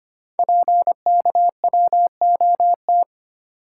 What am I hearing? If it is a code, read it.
Morse code, heard as a keyed tone, about 25 wpm, 700 Hz. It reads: PKWOT
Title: Code Groups - 5 Letters - 25wpm